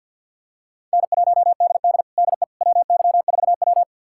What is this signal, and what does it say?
Morse code, heard as a keyed tone, about 52 wpm, 700 Hz. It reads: N1DD DE WY4W
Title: USA Callsign DE Callsign - 4 Characters - 50wpm